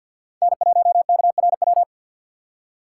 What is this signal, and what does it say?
Morse code, heard as a keyed tone, about 48 wpm, 700 Hz. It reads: N1KRW
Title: USA Call Signs - 5 Characters - 2x Character Spacing - 50wpm